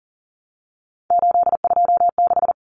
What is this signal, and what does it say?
Morse code, heard as a keyed tone, about 40 wpm, 700 Hz. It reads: 826
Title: Common Cut Numbers - 3-Digit 40wpm